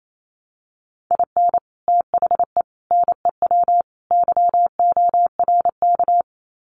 Morse code, heard as a keyed tone, about 28 wpm, 700 Hz. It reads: IN THE NEW YORK
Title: Sets of 4 Words 28wpm